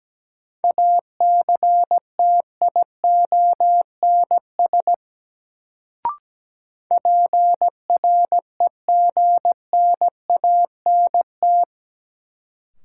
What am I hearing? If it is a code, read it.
Morse code, heard as a keyed tone, about 17 wpm, 700 Hz. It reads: ACTIONS PREGNANT
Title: Sentences from Top 1000 Words - Review Part 6 - New Word Drill - 17wpm